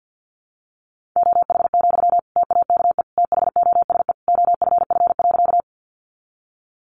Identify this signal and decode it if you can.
Morse code, heard as a keyed tone, about 50 wpm, 700 Hz. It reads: OH, TAKE THOSE OFF.
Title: Sentences from Top 500 Words 50wpm